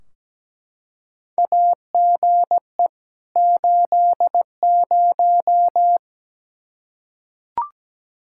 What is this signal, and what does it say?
Morse code, heard as a keyed tone, about 17 wpm, 700 Hz. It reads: AGE 80
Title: K1USN SST QSO Practice 17wpm